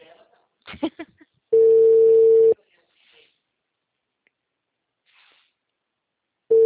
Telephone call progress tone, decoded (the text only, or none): ringback tone